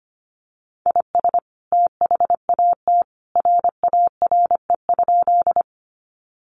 Morse code, heard as a keyed tone, about 25 wpm, 700 Hz. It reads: IS THAT RARE?
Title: Sentences from Top 1100 Words 25wpm